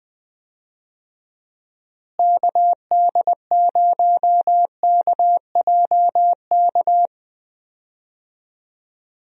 Morse code, heard as a keyed tone, about 20 wpm, 700 Hz. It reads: KD0KJK
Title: USA Call Signs - 6 Characters - Repeated 3x - 20wpm